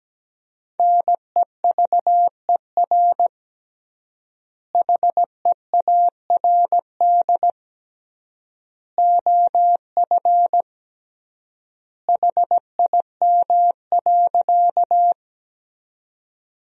Morse code, heard as a keyed tone, about 17 wpm, 700 Hz. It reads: NEVER HEARD OF HIM.
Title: 3x Word-Spacing - Sentences from Top 400 Words - 17wpm